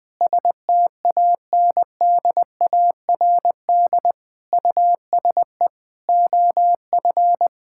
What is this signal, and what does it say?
Morse code, heard as a keyed tone, about 21 wpm, 700 Hz. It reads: STANDARD USE OF
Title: Sentences from Top 1000 Words - Review Part 6 - 20wpm